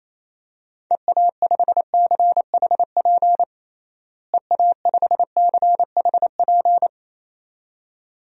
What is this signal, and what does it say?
Morse code, heard as a keyed tone, about 28 wpm, 700 Hz. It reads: EA5CHP EA5CHP
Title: DXCC-100 Call Signs - 6 Characters - Repeated 2x - 28wpm